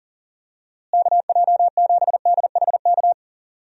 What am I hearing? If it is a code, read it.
Morse code, heard as a keyed tone, about 40 wpm, 700 Hz. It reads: KJ7DHK
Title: USA Call Signs - 6 Characters - Repeated 2x - 40wpm